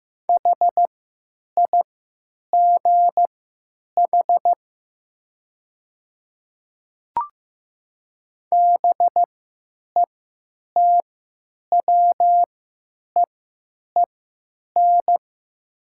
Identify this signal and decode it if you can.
Morse code, heard as a keyed tone, about 15 wpm, spaced as for 9 wpm, 700 Hz. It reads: HIGH BETWEEN
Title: 3x Character Spacing - Top 300 Words - 15wpm